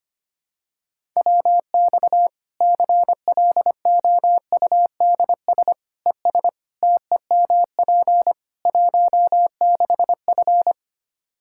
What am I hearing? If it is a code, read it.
Morse code, heard as a keyed tone, about 25 wpm, 700 Hz. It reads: WX CLOUDS ES TEMP 16F
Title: Practice QSO - Exchange 2 - Info - 25wpm